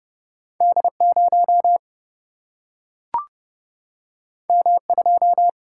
Morse code, heard as a keyed tone, about 30 wpm, 700 Hz. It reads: D 0 M 2
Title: Single-Letter Single-Number 30wpm